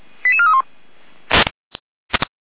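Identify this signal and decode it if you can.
Five-tone selective call (ZVEI), heard as quick runs of five tones, tones of about 70 ms. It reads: 88431
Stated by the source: ZVEI five-tone call 88333 received on SDR.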